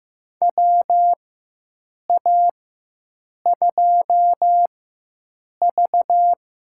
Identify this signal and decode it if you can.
Morse code, heard as a keyed tone, about 15 wpm, spaced as for 7 wpm, 700 Hz. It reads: WA2V
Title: USA Call Signs - 4x Character Spacing - 15wpm